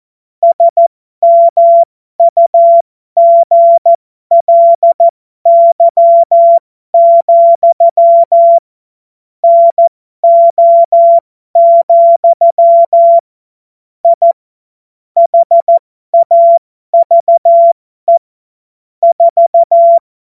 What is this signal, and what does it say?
Morse code, heard as a keyed tone, about 14 wpm, spaced as for 12 wpm, 680 Hz. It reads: SMUGLY, NO, I HAVE 4
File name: Morsekurs / Witze